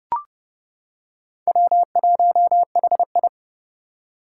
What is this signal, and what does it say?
Morse code, heard as a keyed tone, about 30 wpm, 700 Hz. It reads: W1HI
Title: Four Challenging Letter-Number Combinations 30wpm